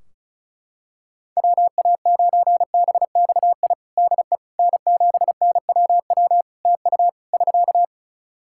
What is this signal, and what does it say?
Morse code, heard as a keyed tone, about 34 wpm, 700 Hz. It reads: WA9BXI DE N7NWW TU <SK>
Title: Practice QSO - Exchange 4 - Ending - 35wpm